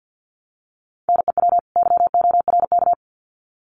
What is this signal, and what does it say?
Morse code, heard as a keyed tone, about 48 wpm, 700 Hz. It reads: NEW YORK
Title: Sets of 2 Words 50wpm